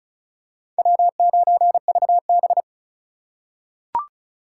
Morse code, heard as a keyed tone, about 36 wpm, 700 Hz. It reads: W9VB
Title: USA Call Signs - 4 Characters - 4x Character Spacing - 35wpm